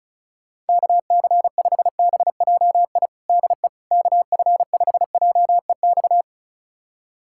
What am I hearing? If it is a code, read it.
Morse code, heard as a keyed tone, about 34 wpm, 700 Hz. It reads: KC5BJI DE KF5JEX
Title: USA Callsign DE Callsign - 6 Characters - 35wpm